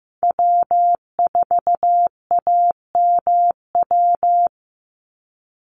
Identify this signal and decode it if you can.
Morse code, heard as a keyed tone, about 14 wpm, 700 Hz. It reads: W4AMW
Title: NA SOTA Chaser Callsigns 15wpm